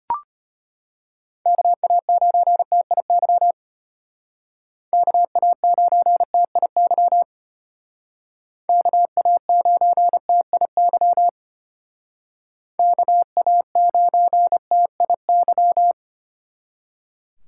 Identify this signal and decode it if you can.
Morse code, heard as a keyed tone, about 32 wpm, 700 Hz. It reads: KA9TIY KA9TIY KA9TIY KA9TIY